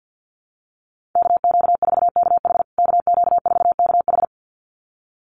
Morse code, heard as a keyed tone, about 48 wpm, 700 Hz. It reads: KQ4KH KQ4KH
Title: USA Call Signs Repeated 2x 50wpm